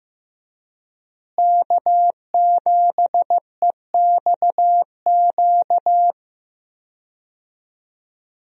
Morse code, heard as a keyed tone, about 15 wpm, 700 Hz. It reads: K7EXQ
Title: USA Call Signs - 5 Characters - Repeated 3x - 15wpm